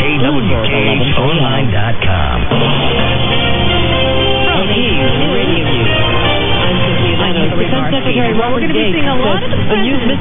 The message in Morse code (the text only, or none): none